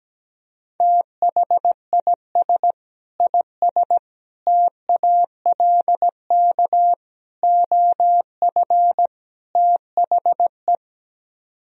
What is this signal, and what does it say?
Morse code, heard as a keyed tone, about 17 wpm, 700 Hz. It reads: THIS IS TALK OF THE
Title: Speed-Racing - Sets of 5 Words 17wpm